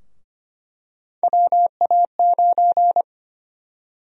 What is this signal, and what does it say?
Morse code, heard as a keyed tone, about 25 wpm, 700 Hz. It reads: WA9
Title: USA Call Sign Prefixes - 25wpm